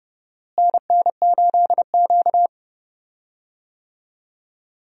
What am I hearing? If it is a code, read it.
Morse code, heard as a keyed tone, about 30 wpm, 700 Hz. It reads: NN8Q
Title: USA Call Signs - 4 Characters - Repeated 2x - 30wpm